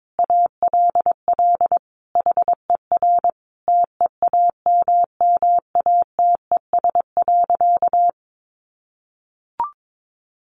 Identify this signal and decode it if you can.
Morse code, heard as a keyed tone, about 22 wpm, 700 Hz. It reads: ALL HER TEAMMATES.
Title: Easy Phrases 22wpm